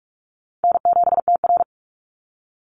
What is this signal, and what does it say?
Morse code, heard as a keyed tone, about 45 wpm, 700 Hz. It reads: N7TR
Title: Call Signs Repeated 2x - 45wpm